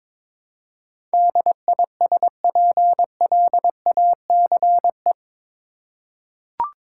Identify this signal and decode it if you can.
Morse code, heard as a keyed tone, about 22 wpm, 700 Hz. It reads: DISPLACE
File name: practice / 22wpm / Eight Letter Words - 22wpm